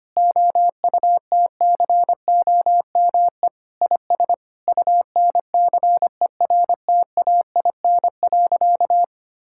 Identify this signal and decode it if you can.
Morse code, heard as a keyed tone, about 25 wpm, 700 Hz. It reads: OUTCOME IS UNCERTAIN.
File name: practice / 25wpm / Sentences with 3-Word Phrases - 25wpm